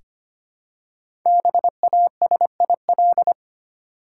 Morse code, heard as a keyed tone, about 25 wpm, 700 Hz. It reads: BASIL
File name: practice / 25wpm / Easy Words with Hints - 25wpm